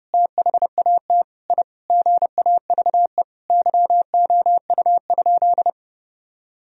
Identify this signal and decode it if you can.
Morse code, heard as a keyed tone, about 30 wpm, 700 Hz. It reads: THAT I GAVE YOU?
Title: Sentences from Top 1000 Words - Review Part 4 - 30wpm